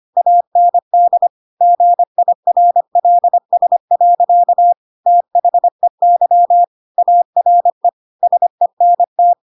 Morse code, heard as a keyed tone, about 25 wpm, 700 Hz. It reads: AND GIRLS. THEY ARE SENT
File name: Pinocchio / Pinocchio ch4 25 25 ik1hge